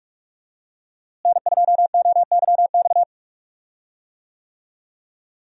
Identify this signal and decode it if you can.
Morse code, heard as a keyed tone, about 46 wpm, 700 Hz. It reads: N2OYX
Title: USA Call Signs - 5 Characters - Repeated 2x - 45wpm